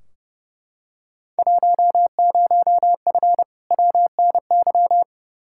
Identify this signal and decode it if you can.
Morse code, heard as a keyed tone, about 31 wpm, 700 Hz. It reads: 10F WNY
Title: Field Day Exchanges Encore - 30wpm